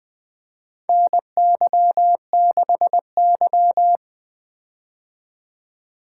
Morse code, heard as a keyed tone, about 20 wpm, 700 Hz. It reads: NY6Y
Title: K1USN SST QSO Practice - 20wpm